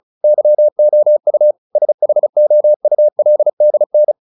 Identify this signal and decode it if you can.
Morse code, heard as a keyed tone, about 34 wpm, 600 Hz. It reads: YOU SHOULDN
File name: the old man and the sea / the old man and the sea 1 35wpm